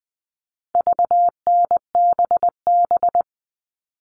Morse code, heard as a keyed tone, about 20 wpm, 700 Hz. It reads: VNBB
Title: All Cut Numbers - 4-Digit 20wpm